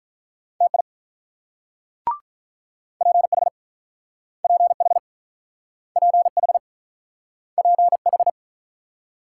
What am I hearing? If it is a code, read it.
Morse code, heard as a keyed tone, about 44 wpm, 700 Hz. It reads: TI PH PH PH PH